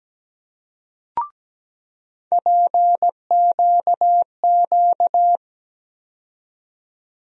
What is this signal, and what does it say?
Morse code, heard as a keyed tone, about 17 wpm, 700 Hz. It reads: PQQ
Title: USA Call Sign Postfixes - 3 Letters 17wpm